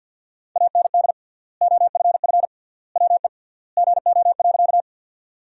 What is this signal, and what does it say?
Morse code, heard as a keyed tone, about 50 wpm, 700 Hz. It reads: AND OFF WE GO.
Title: 3x Word-Spacing - Sentences from Top 200 Words - 50wpm